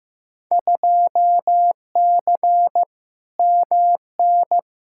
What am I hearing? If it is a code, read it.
Morse code, heard as a keyed tone, about 15 wpm, 700 Hz. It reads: 2C MN
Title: Field Day Exchanges Encore - 15wpm